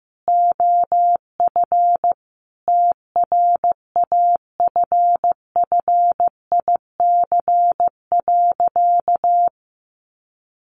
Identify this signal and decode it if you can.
Morse code, heard as a keyed tone, about 14 wpm, 700 Hz. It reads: OF TRAFFIC.
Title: Sentences from Top 1300 Words 15wpm